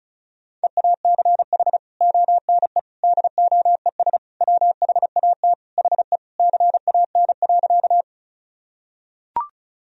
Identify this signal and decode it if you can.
Morse code, heard as a keyed tone, about 34 wpm, 700 Hz. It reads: EACH ONE DOES WHAT HE CAN.
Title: Sentences from Top 1000 Words - Review Part 1 - 35wpm